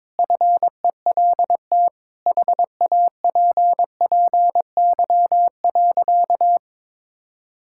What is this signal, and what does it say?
Morse code, heard as a keyed tone, about 22 wpm, 700 Hz. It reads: FELT HAPPY.
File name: practice / 22wpm / Sentences with 8-Word Phrases - 22wpm